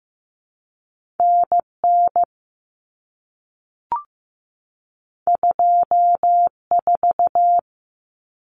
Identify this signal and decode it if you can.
Morse code, heard as a keyed tone, about 15 wpm, 700 Hz. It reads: NN 24
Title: Common Cut Numbers - 2-Digit 15wpm